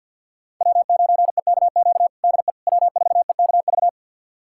Morse code, heard as a keyed tone, about 52 wpm, 700 Hz. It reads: W9EKQ DE W4EKV
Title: USA Callsign DE Callsign - 5 Characters - 50wpm